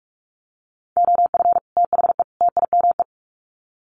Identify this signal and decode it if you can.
Morse code, heard as a keyed tone, about 44 wpm, 700 Hz. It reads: OF THE TIME
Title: Sets of 3 Words 45wpm